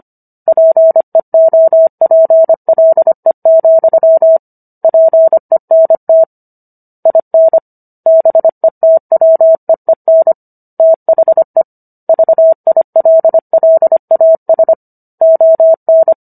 Morse code, heard as a keyed tone, about 25 wpm, 650 Hz. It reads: PEOPLE, PENT IN BETWEEN THE VILLAS ON